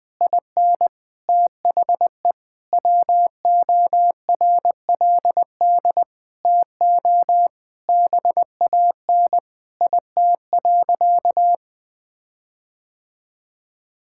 Morse code, heard as a keyed tone, about 20 wpm, 700 Hz. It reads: IN THE WORLD TO BAN IT.
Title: Sentences with 8-Word Phrases - Spoken Beginning - 20wpm